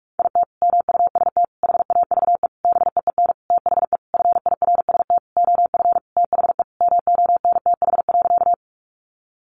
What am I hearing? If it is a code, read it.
Morse code, heard as a keyed tone, about 44 wpm, 700 Hz. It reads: IT MUST HAVE BEEN THE FIRST OF THE MONTH.
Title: Sentences from Top 800 Words 45wpm